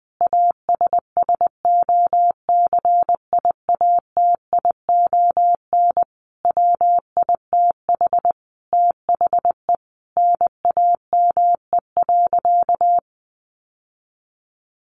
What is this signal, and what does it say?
Morse code, heard as a keyed tone, about 20 wpm, 700 Hz. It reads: ASSOCIATION WITH THE NAME.
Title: Sentences from Top 1100 Words 20wpm